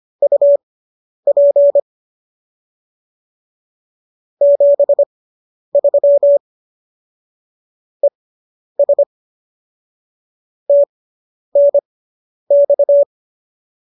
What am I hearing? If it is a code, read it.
Morse code, heard as a keyed tone, about 25 wpm, spaced as for 10 wpm, 570 Hz. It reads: UP 73 ES TNX